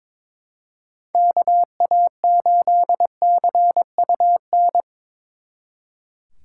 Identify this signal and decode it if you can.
Morse code, heard as a keyed tone, about 26 wpm, 700 Hz. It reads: KA8CUN
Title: Speed Racing - USA 2x3 Call Signs 22wpm